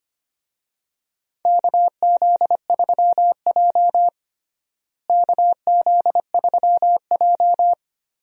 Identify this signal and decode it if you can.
Morse code, heard as a keyed tone, about 25 wpm, 700 Hz. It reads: KZ3J KZ3J
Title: USA Call Signs - 4 Characters - Repeated 3x - 25wpm